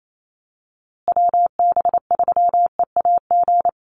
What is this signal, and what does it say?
Morse code, heard as a keyed tone, about 28 wpm, 700 Hz. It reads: WB3EAG
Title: USA Call Signs Repeated 3x 28wpm